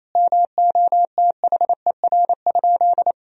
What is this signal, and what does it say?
Morse code, heard as a keyed tone, about 28 wpm, 700 Hz. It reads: MOTHER?
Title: Sentences from Top 1000 Words - Review Part 2 - 28wpm